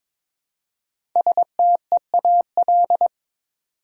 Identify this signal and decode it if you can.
Morse code, heard as a keyed tone, about 22 wpm, 700 Hz. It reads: STEAL
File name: 5 Letter Words - First 3 Letters Given - 22wpm